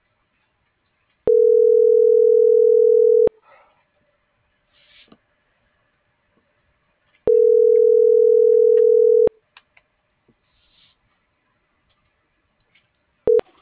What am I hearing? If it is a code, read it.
Ringback tone (North America), 440 + 480 Hz.